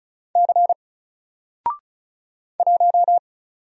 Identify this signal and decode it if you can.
Morse code, heard as a keyed tone, about 36 wpm, 700 Hz. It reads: C 1
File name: Single-Letter-Number-35wpm